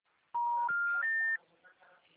Special information tone.
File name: out- -2002- - - .235